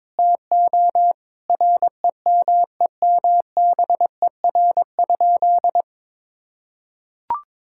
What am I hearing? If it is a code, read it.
Morse code, heard as a keyed tone, about 22 wpm, 700 Hz. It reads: TO REMEMBER?
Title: Sentences from Top 1000 Words - Review Part 6 - 22wpm